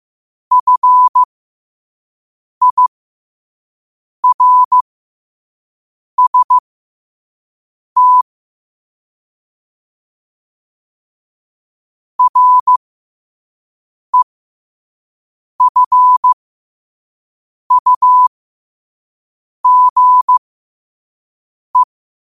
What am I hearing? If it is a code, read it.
Morse code, heard as a keyed tone, about 15 wpm, spaced as for 5 wpm, 1000 Hz. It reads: FIRST REFUGE
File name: qotdfiles / 5wpm